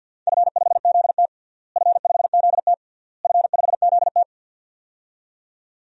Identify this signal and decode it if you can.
Morse code, heard as a keyed tone, about 49 wpm, 700 Hz. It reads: F5ZT F5ZT F5ZT
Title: DXCC-100 Call Signs - 4 Characters - Repeated 3x - 50wpm